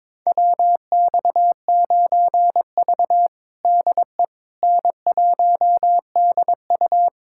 Morse code, heard as a keyed tone, about 22 wpm, 700 Hz. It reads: WX9V DE N1DU
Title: USA Callsign DE Callsign - 4 Characters - 22wpm